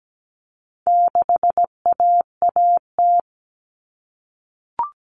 Morse code, heard as a keyed tone, about 17 wpm, 700 Hz. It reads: 6AAT
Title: Common Cut Numbers - 4-Digit 17wpm